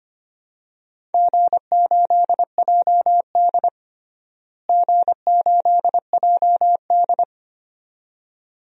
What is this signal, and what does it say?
Morse code, heard as a keyed tone, about 25 wpm, 700 Hz. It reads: G8JD G8JD
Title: DXCC-100 Call Signs - 4 Characters - Repeated 2x - 25wpm